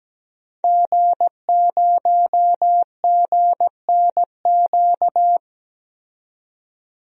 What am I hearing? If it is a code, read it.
Morse code, heard as a keyed tone, about 17 wpm, 700 Hz. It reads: G0GNQ
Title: DXCC-100 Call Signs - 5 Characters - 17wpm